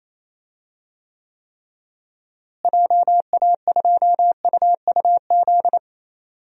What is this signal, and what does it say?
Morse code, heard as a keyed tone, about 27 wpm, 700 Hz. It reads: JA2UUZ
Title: DXCC-100 Call Signs - 6 Characters - Repeated 3x - 28wpm